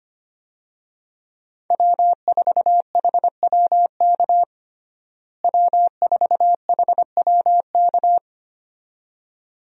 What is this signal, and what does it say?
Morse code, heard as a keyed tone, about 25 wpm, 700 Hz. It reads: W4HWK W4HWK
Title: USA Call Signs - 5 Characters - Repeated 2x - 25wpm